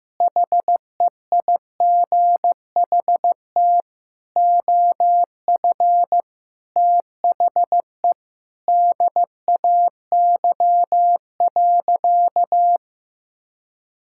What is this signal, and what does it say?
Morse code, heard as a keyed tone, about 15 wpm, 700 Hz. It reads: HEIGHT OF THE DAY.
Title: Sentences from Top 1600 Words 15wpm